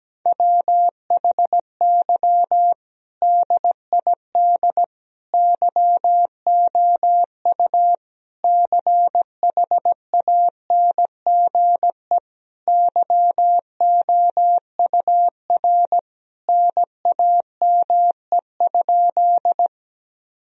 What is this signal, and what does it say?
Morse code, heard as a keyed tone, about 17 wpm, 700 Hz. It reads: WHY DID YOU CHANGE YOUR NAME?
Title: Sentences from Top 1000 Words - Review Part 1 - 17wpm